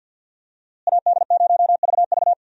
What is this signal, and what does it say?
Morse code, heard as a keyed tone, about 51 wpm, 700 Hz. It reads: AD0VV
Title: USA Call Signs - 5 Characters - Repeated 2x - 50wpm